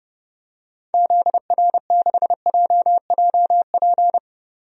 Morse code, heard as a keyed tone, about 30 wpm, 700 Hz. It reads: ZR6JJP
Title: DXCC-100 Call Signs - 6 Characters - 30wpm